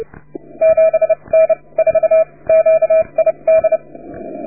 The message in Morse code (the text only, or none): 7N4QID